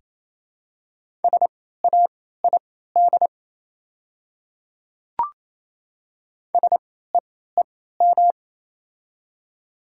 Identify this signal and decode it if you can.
Morse code, heard as a keyed tone, about 28 wpm, spaced as for 16 wpm, 700 Hz. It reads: SAID SEEM
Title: 3x Character Spacing - Top 300 Words - 28wpm